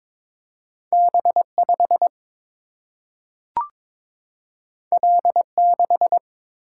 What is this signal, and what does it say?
Morse code, heard as a keyed tone, about 22 wpm, 700 Hz. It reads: B5 L6
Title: Single-Letter Single-Number 22wpm